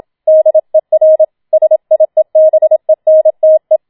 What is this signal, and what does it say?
Morse code, heard as a keyed tone, about 27 wpm, 630 Hz. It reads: DER SIEBENTE
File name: Schneewittchen Teil2 27wpm